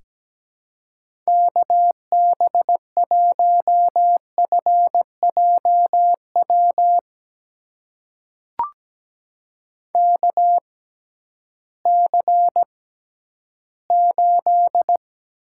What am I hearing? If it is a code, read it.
Morse code, heard as a keyed tone, about 17 wpm, spaced as for 11 wpm, 700 Hz. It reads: KB1FJW KC8
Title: USA Call Signs - 6 Characters - 6x Character Spacing - 17wpm